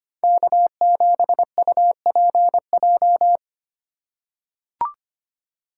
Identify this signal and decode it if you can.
Morse code, heard as a keyed tone, about 25 wpm, 700 Hz. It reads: K7UPJ
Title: K1USN SST QSO Practice - 25wpm